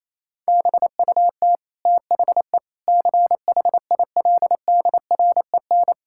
Morse code, heard as a keyed tone, about 27 wpm, 700 Hz. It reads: BUT THE CHILDREN
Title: Sentences from Top 1000 Words - Review Part 6 - 28wpm